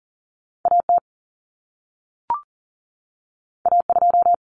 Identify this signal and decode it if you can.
Morse code, heard as a keyed tone, about 40 wpm, 700 Hz. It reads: A T A 2
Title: Common Cut Numbers - 2-Digit 40wpm